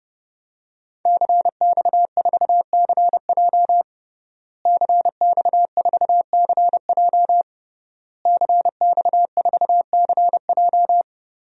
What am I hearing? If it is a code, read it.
Morse code, heard as a keyed tone, about 30 wpm, 700 Hz. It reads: CX4CJ CX4CJ CX4CJ
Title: DXCC-100 Call Signs - 5 Characters - Repeated 3x - 30wpm